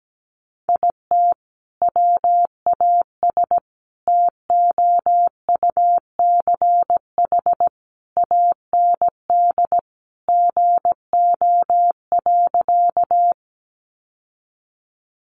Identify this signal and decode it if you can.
Morse code, heard as a keyed tone, about 17 wpm, 700 Hz. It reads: IT WAS TOUCH AND GO.
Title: Sentences from Top 1200 Words 17wpm